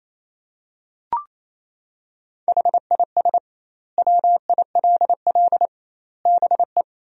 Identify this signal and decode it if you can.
Morse code, heard as a keyed tone, about 28 wpm, 700 Hz. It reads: HIS WILL BE
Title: 2x Word-Spacing - Sentences from Top 500 Words - 28wpm